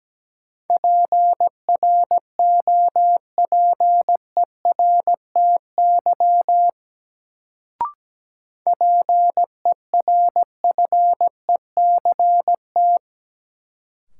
Morse code, heard as a keyed tone, about 17 wpm, 700 Hz. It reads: PROPERTY PERFECT
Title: Sentences from Top 1000 Words - New Word Drill - 17wpm